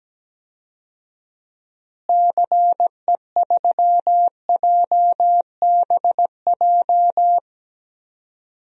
Morse code, heard as a keyed tone, about 17 wpm, 700 Hz. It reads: CE3JBJ
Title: DXCC-100 Call Signs - 6 Characters - Repeated 2x - 17wpm